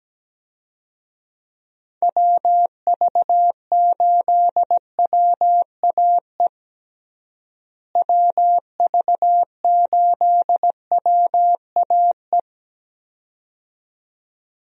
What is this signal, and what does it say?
Morse code, heard as a keyed tone, about 17 wpm, 700 Hz. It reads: WV8WAE WV8WAE
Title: USA Call Signs - 6 Characters - Repeated 2x - 17wpm